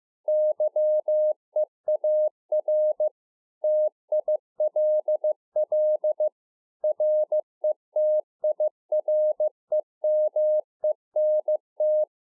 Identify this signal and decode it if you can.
Morse code, heard as a keyed tone, about 15 wpm, 610 Hz. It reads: YEAR TILL RETIREMENT